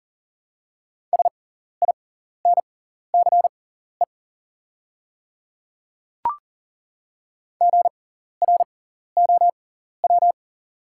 Morse code, heard as a keyed tone, about 40 wpm, spaced as for 14 wpm, 700 Hz. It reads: SINCE GROW
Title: 6x Character Spacing - Top 300 Words - 40wpm